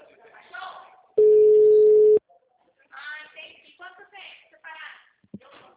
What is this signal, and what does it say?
Ringback tone (Europe, 425 Hz), with speech.
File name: out- -2009- - - .22636